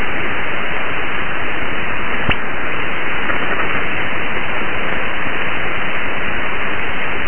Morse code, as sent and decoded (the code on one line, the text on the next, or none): none